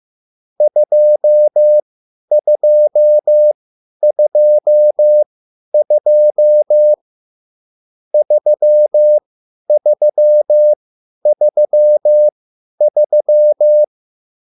Morse code, heard as a keyed tone, about 15 wpm, spaced as for 10 wpm, 600 Hz. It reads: 2222 3333
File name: QSL / CW lesson